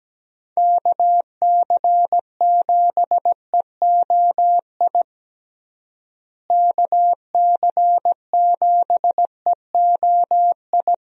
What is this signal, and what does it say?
Morse code, heard as a keyed tone, about 17 wpm, 700 Hz. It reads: KC7EOI KC7EOI
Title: USA Call Signs - 6 Characters - Repeated 2x - 17wpm